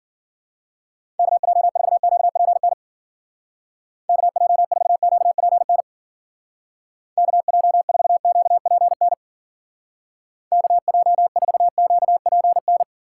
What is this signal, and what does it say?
Morse code, heard as a keyed tone, about 50 wpm, 700 Hz. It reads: KJ4QPN KJ4QPN KJ4QPN KJ4QPN
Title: Speed Racing - USA 2x3 Call Signs 40wpm